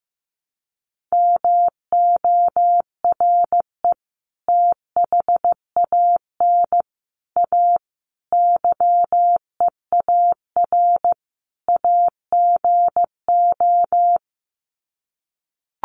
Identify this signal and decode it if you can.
Morse code, heard as a keyed tone, about 15 wpm, 700 Hz. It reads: MORE THAN A YEAR AGO
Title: Sets of 5 Words Encore2 15wpm